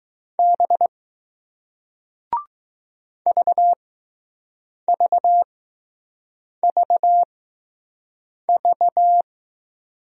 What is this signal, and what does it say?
Morse code, heard as a keyed tone, about 20 wpm, spaced as for 6 wpm, 700 Hz. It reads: B VVVV